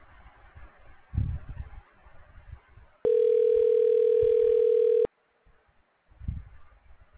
Ringback tone (North America), 440 + 480 Hz.